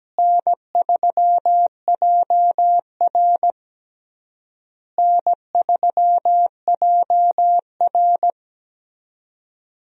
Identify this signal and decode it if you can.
Morse code, heard as a keyed tone, about 17 wpm, 700 Hz. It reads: N3JR N3JR
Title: USA Call Signs - 4 Characters - Repeated 3x - 17wpm